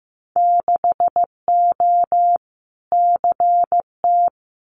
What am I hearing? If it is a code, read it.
Morse code, heard as a keyed tone, about 15 wpm, 700 Hz. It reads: 6O CT
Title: Winter Field Day Exchanges Encore 15wpm